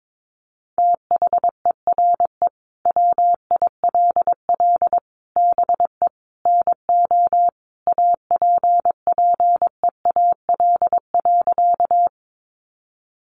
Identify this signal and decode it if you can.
Morse code, heard as a keyed tone, about 22 wpm, 700 Hz. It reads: THERE WILL BE NO APPEAL.
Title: Sentences from Top 1300 Words 22wpm